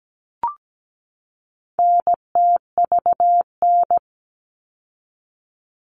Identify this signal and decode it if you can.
Morse code, heard as a keyed tone, about 17 wpm, 700 Hz. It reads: NTVN
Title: All Cut Numbers - 4-Digit 17wpm